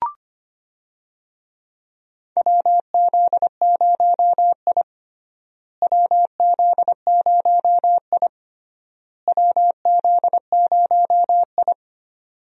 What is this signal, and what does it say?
Morse code, heard as a keyed tone, about 25 wpm, 700 Hz. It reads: WZ0I WZ0I WZ0I
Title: USA Call Signs - 4 Characters - Repeated 3x - 25wpm